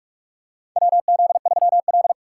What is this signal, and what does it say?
Morse code, heard as a keyed tone, about 47 wpm, 700 Hz. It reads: WZ3L
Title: USA Call Signs - 4 Characters - 6x Character Spacing - 45wpm